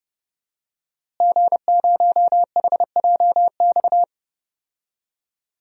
Morse code, heard as a keyed tone, about 30 wpm, 700 Hz. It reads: G0HJX
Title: DXCC-100 Call Signs - 5 Characters - Repeated 2x - 30wpm